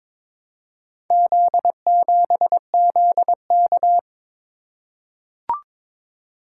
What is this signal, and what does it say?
Morse code, heard as a keyed tone, about 22 wpm, 700 Hz. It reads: Z7ZK
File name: practice / 22wpm / Four Letter-Number Combinations - 22wpm